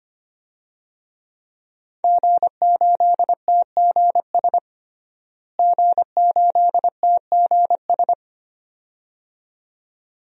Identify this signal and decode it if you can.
Morse code, heard as a keyed tone, about 25 wpm, 700 Hz. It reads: G8TGS G8TGS
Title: DXCC-100 Call Signs - 5 Characters - Repeated 2x - 25wpm